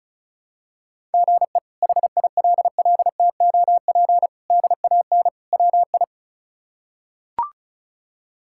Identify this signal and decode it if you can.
Morse code, heard as a keyed tone, about 34 wpm, 700 Hz. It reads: GE HILLTOP DAN WI
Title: K1USN SST QSO Practice - 35wpm